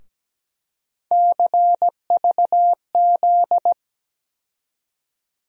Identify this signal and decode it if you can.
Morse code, heard as a keyed tone, about 17 wpm, 700 Hz. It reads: CVZ
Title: USA Call Sign Postfixes - 3 Letters 17wpm